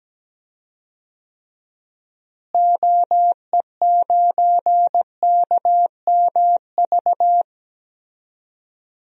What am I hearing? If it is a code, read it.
Morse code, heard as a keyed tone, about 17 wpm, 700 Hz. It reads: OE9KMV
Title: DXCC-100 Call Signs - 6 Characters - Repeated 3x - 17wpm